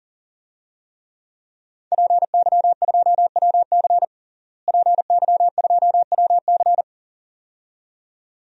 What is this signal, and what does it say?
Morse code, heard as a keyed tone, about 40 wpm, 700 Hz. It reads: PY2WC PY2WC
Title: DXCC-100 Call Signs - 5 Characters - Repeated 2x - 40wpm